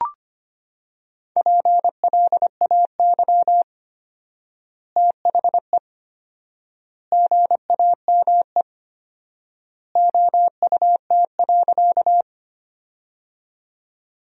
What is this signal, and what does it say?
Morse code, heard as a keyed tone, about 25 wpm, 700 Hz. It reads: PLAY THE GAME OUT.
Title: 4x Word-Spacing - Sentences from Top 500 Words - 25wpm